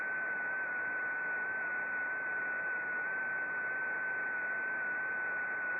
FT8.
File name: contests / sept-vhf-k5tr- -0300